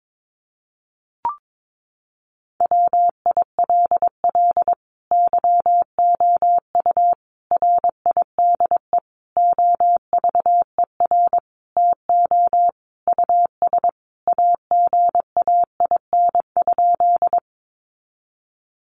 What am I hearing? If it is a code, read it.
Morse code, heard as a keyed tone, about 22 wpm, 700 Hz. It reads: WILL YOU RIDE OVER TO US AGAIN?
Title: Sentences from Top 1200 Words 22wpm